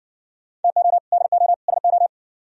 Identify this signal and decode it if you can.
Morse code, heard as a keyed tone, about 54 wpm, 700 Hz. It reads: TO DO SO